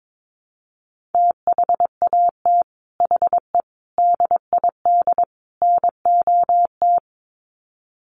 Morse code, heard as a keyed tone, about 22 wpm, 700 Hz. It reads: THAT HE DID NOT
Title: Sets of 4 Words 22wpm